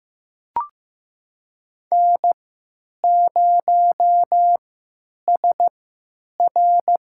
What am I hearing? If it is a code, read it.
Morse code, heard as a keyed tone, about 15 wpm, spaced as for 9 wpm, 700 Hz. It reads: N0SR
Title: USA Call Signs - 3x Character Spacing - 15wpm